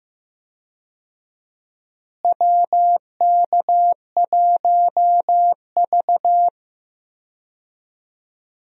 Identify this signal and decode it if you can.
Morse code, heard as a keyed tone, about 16 wpm, 700 Hz. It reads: WK1V
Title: USA Call Signs - 4 Characters - Repeated 2x - 15wpm